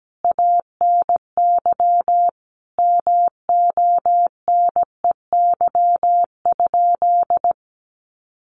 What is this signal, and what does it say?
Morse code, heard as a keyed tone, about 17 wpm, 700 Hz. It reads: ANY MONEY?
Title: Sentences from Top 400 Words 17wpm